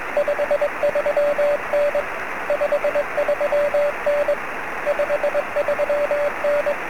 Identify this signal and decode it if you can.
Morse code, heard as a keyed tone, about 21 wpm, 580 Hz, with radio noise. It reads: 53N 53N 53N
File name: sm7lcb / 02-oz1ff 23cm